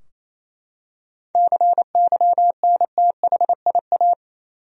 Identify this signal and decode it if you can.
Morse code, heard as a keyed tone, about 28 wpm, 700 Hz. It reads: CYNTHIA
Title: USA Common Names - 28wpm